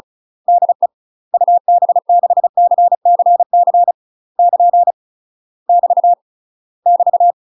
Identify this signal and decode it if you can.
Morse code, heard as a keyed tone, about 35 wpm, 700 Hz. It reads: DE UB6CCC <KN> <BT> <BT>
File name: UK MASTER QSOs TRAINING / 35WPM / UK MASTER QSOs 0010